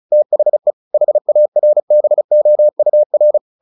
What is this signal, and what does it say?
Morse code, heard as a keyed tone, about 34 wpm, 600 Hz. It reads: THE HARBOUR
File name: the old man and the sea 2 35wpm